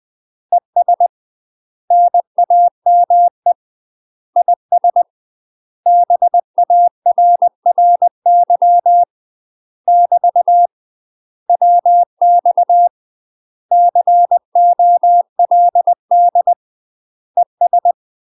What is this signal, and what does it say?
Morse code, heard as a keyed tone, about 20 wpm, 700 Hz. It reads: ES NAME IS BARRY <BT> WX COLD ES
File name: UK MASTER QSOs TRAINING / 20WPM / UK MASTER QSOs 0005